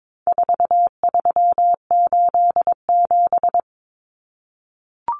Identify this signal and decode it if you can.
Morse code, heard as a keyed tone, about 22 wpm, 700 Hz. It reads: 4387
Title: Numbers - 4-Digit 22wpm